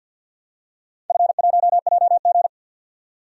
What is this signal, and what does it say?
Morse code, heard as a keyed tone, about 46 wpm, 700 Hz. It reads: F1JG
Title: DXCC-100 Call Signs - 4 Characters - 50wpm